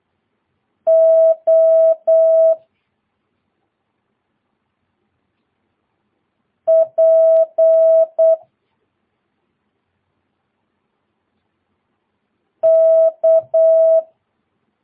Morse code, heard as a keyed tone, about 8 wpm, 650 Hz. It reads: O P K
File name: jh8cbh / 2022cw